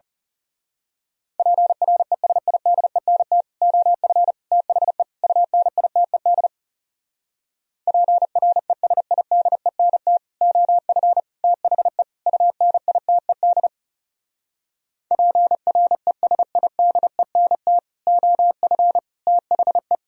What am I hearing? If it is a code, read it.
Morse code, heard as a keyed tone, about 34 wpm, 700 Hz. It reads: PRESIDENT OF THE UNITED PRESIDENT OF THE UNITED PRESIDENT OF THE
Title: Speed-Racing - Sets of 4 Words 30wpm